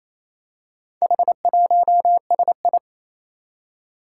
Morse code, heard as a keyed tone, about 27 wpm, 700 Hz. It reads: H1SI
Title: Four Challenging Letter-Number Combinations 28wpm